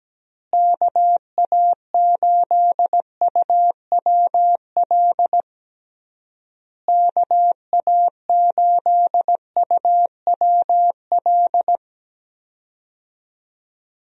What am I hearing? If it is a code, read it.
Morse code, heard as a keyed tone, about 17 wpm, 700 Hz. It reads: KA8UWL KA8UWL
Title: USA Call Signs - 6 Characters - Repeated 3x - 17wpm